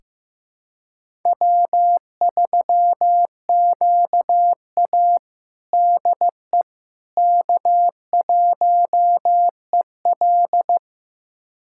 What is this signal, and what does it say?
Morse code, heard as a keyed tone, about 15 wpm, 700 Hz. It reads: W3QA DE K1EL
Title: USA Callsign DE Callsign - 4 Characters - 15wpm